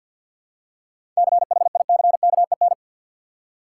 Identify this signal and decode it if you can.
Morse code, heard as a keyed tone, about 51 wpm, 700 Hz. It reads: CHICKEN
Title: Seven Letter Words - 50wpm